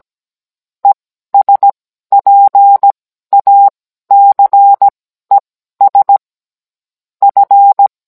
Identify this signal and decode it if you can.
Morse code, heard as a keyed tone, about 17 wpm, spaced as for 12 wpm, 800 Hz. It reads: ESPACES F